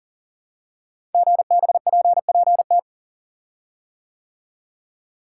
Morse code, heard as a keyed tone, about 42 wpm, 700 Hz. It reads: GBPPT
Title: Code Groups - 5 Letters - 40wpm